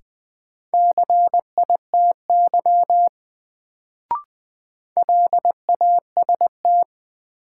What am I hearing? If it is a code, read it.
Morse code, heard as a keyed tone, about 20 wpm, 700 Hz. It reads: CITY LAST
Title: Sentences from Top 300 Words - New Word Drill - 20wpm